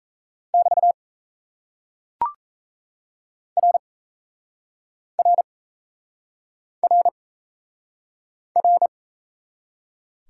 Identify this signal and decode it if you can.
Morse code, heard as a keyed tone, about 38 wpm, 700 Hz. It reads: <BT> R R R R